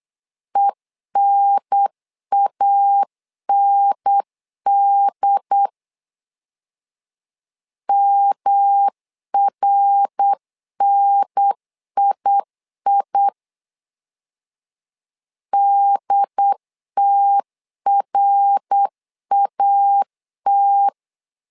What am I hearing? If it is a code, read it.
Morse code, heard as a keyed tone, about 8 wpm, 800 Hz. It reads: ENAND MRNII DTRAT
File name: LEZIONI / 3 MWRO / 3 MWR 8wpm 2444